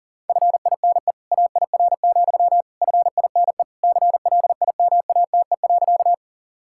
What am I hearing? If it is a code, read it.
Morse code, heard as a keyed tone, about 40 wpm, 700 Hz. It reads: FINE AIR, FINE CLIMATE.
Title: Sentences from Top 1600 Words 40wpm